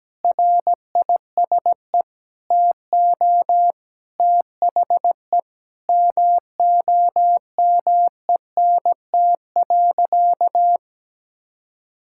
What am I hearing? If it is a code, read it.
Morse code, heard as a keyed tone, about 17 wpm, 700 Hz. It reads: RISE TO THE MOMENT.
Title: Sentences from Top 1000 Words - Review Part 6 - 17wpm